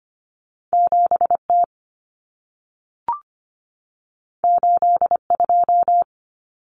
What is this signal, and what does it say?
Morse code, heard as a keyed tone, about 25 wpm, 700 Hz. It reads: 7T 82
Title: Common Cut Numbers - 2-Digit 25wpm